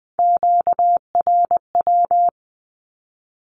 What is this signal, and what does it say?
Morse code, heard as a keyed tone, about 20 wpm, 700 Hz. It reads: QRW
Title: Q-Codes with Spoken Meaning - 20wpm